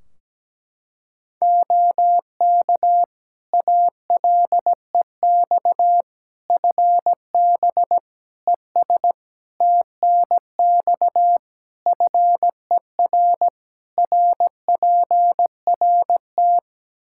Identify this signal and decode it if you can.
Morse code, heard as a keyed tone, about 17 wpm, 700 Hz. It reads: OK ALEX FB ES TNX FER RPRT
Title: Practice QSO - Exchange 2 - Info - 17wpm